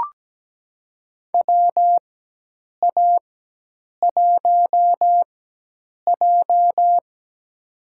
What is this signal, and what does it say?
Morse code, heard as a keyed tone, about 17 wpm, spaced as for 8 wpm, 700 Hz. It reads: WA1J
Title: USA Call Signs - 4 Characters - 4x Character Spacing - 17wpm